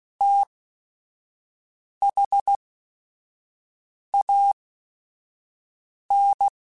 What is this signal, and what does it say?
Morse code, heard as a keyed tone, about 16 wpm, spaced as for 5 wpm, 800 Hz. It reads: THAN
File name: Technical Subjects / Technical Subjects-mp3s / Sw Pwr Sup @5